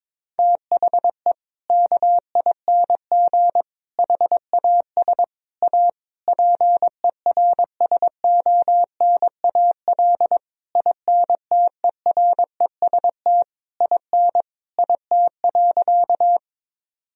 Morse code, heard as a keyed tone, about 22 wpm, 700 Hz. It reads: THE KING HAS A PERSONAL INTEREST IN IT.
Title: Sentences from Top 1000 Words - Review Part 4 - 22wpm